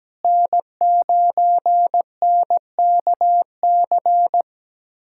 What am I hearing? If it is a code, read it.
Morse code, heard as a keyed tone, about 17 wpm, 700 Hz. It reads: N9NKC
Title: USA Callsign DE Callsign - 5 Characters - 17wpm